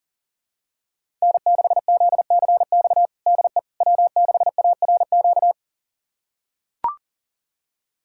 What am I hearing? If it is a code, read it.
Morse code, heard as a keyed tone, about 40 wpm, 700 Hz. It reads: N6ZCX DE W6ARQ
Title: USA Callsign DE Callsign - 5 Characters - 40wpm